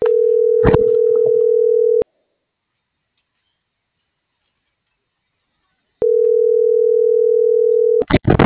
Ringback tone (North America), 440 + 480 Hz.